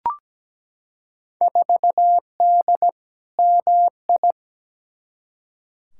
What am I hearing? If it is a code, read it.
Morse code, heard as a keyed tone, about 17 wpm, 700 Hz. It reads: 4D MI
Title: Field Day Exchanges Encore - 17wpm